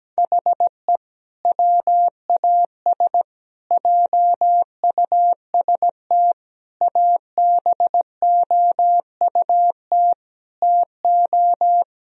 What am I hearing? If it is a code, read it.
Morse code, heard as a keyed tone, about 17 wpm, 700 Hz. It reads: HE WAS JUST ABOUT TO
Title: Sentences from Top 1000 Words - Review Part 2 - 17wpm